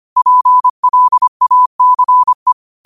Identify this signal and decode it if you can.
Morse code, heard as a keyed tone, about 25 wpm, 1000 Hz. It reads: PLACE
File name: qotdfiles / 25wpm